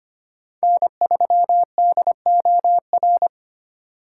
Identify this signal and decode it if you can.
Morse code, heard as a keyed tone, about 25 wpm, 700 Hz. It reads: N3DOR
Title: USA Call Signs - 5 Characters - 2x Character Spacing - 25wpm